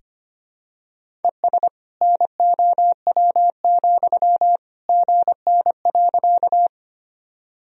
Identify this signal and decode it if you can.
Morse code, heard as a keyed tone, about 25 wpm, 700 Hz. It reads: ES NOW, GN.
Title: Sentences from Top 300 Words and Abbreviations 25wpm